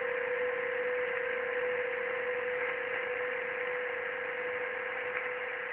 PSK31, about 31.25 baud.
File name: ARCHIV / BPSK CW / OK1IF-OK1ANG- -1230CET-3766.5-3767